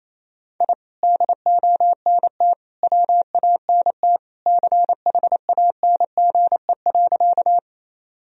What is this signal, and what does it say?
Morse code, heard as a keyed tone, about 28 wpm, 700 Hz. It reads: I DONT WANT CHANGE.
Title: Sentences from Top 1000 Words - Review Part 1 - 28wpm